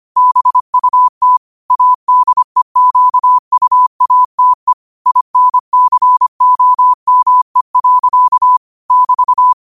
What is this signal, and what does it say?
Morse code, heard as a keyed tone, about 25 wpm, 1000 Hz. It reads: BUT ADEQUATE INCOME. <BT>
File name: qotdfiles / 25wpm